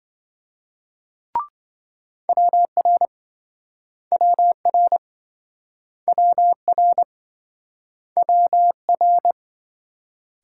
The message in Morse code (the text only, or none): WR WR WR WR